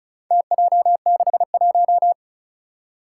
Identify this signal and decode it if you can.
Morse code, heard as a keyed tone, about 36 wpm, 700 Hz. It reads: TJ61
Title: Four Challenging Letter-Number Combinations 35wpm